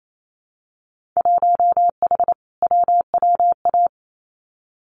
Morse code, heard as a keyed tone, about 28 wpm, 690 Hz. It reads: 1H WWA
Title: Winter Field Day Exchanges Encore 28wpm